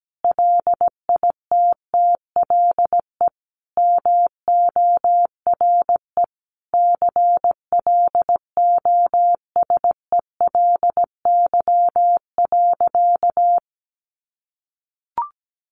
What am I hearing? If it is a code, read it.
Morse code, heard as a keyed tone, about 17 wpm, 700 Hz. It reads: LITTLE MORE CLOSELY.
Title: Sentences from Top 1400 Words 17wpm